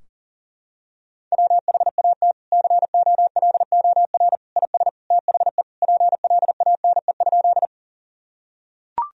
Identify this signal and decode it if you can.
Morse code, heard as a keyed tone, about 40 wpm, 700 Hz. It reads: WHAT COLOR IS THE PLANE?
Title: Sentences from Top 1000 Words - Review Part 3 - 40wpm